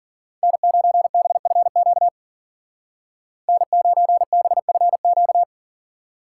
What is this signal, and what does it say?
Morse code, heard as a keyed tone, about 50 wpm, 700 Hz. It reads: N9BFQ N9BFQ
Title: Speed Racing - USA 1x3 Call Signs 40wpm